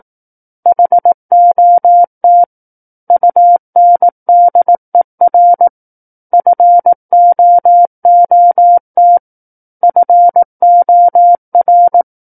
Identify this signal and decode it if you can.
Morse code, heard as a keyed tone, about 18 wpm, 700 Hz. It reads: HOT UNDER FOOT FOR
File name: 18 WPM